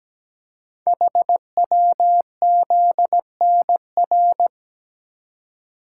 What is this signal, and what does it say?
Morse code, heard as a keyed tone, about 17 wpm, 700 Hz. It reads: HWZNR
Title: Code Groups - 5 Letter:Numbers - 17wpm